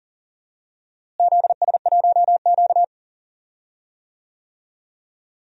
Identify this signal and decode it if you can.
Morse code, heard as a keyed tone, about 40 wpm, 700 Hz. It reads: ZS1Q
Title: DXCC-100 Call Signs - 4 Characters - Repeated 3x - 40wpm